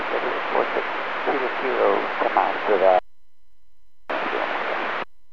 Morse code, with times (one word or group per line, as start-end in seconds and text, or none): none